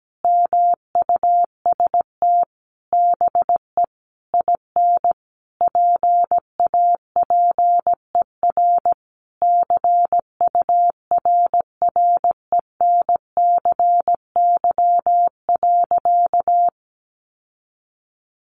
Morse code, heard as a keyed tone, about 17 wpm, 700 Hz. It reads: MUST BE IN PAPER CURRENCY.
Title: Sentences from Top 1500 Words 17wpm